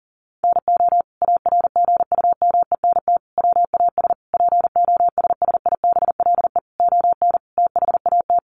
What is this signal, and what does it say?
Morse code, heard as a keyed tone, about 37 wpm, 700 Hz. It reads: NO ARGUMENT WAS POSSIBLE ON THAT
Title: Sentences from Top 1400 Words 40wpm